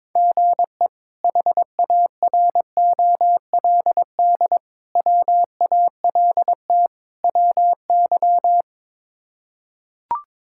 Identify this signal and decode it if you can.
Morse code, heard as a keyed tone, about 22 wpm, 700 Hz. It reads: GE HAROLD WALT WY
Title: K1USN SST QSO Practice - 22wpm